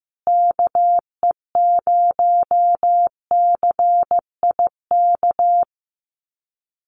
Morse code, heard as a keyed tone, about 15 wpm, 700 Hz. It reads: KE0CIK
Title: USA Call Signs Repeated 3x 15wpm